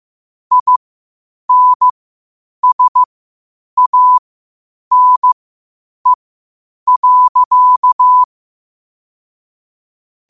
Morse code, heard as a keyed tone, about 15 wpm, spaced as for 8 wpm, 1000 Hz. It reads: INSANE.
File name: qotdfiles / 8wpm